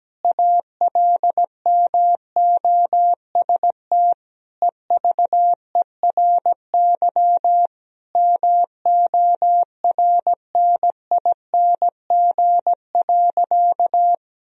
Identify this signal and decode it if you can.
Morse code, heard as a keyed tone, about 17 wpm, 700 Hz. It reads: ALMOST EVERY MORNING.
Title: Sentences from Top 1000 Words - Review Part 4 - 17wpm